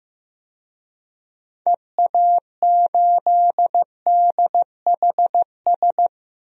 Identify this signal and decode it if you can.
Morse code, heard as a keyed tone, about 15 wpm, 700 Hz. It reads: EA8DHS
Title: DXCC-100 Call Signs - 6 Characters - Repeated 2x - 15wpm